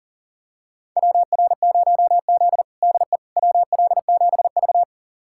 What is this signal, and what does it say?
Morse code, heard as a keyed tone, about 40 wpm, 700 Hz. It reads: WR0Z DE WL7V
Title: USA Callsign DE Callsign - 4 Characters - 40wpm